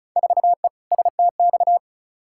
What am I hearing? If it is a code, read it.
Morse code, heard as a keyed tone, about 36 wpm, 700 Hz. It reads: 4E STX
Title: Field Day Exchanges Encore - 35wpm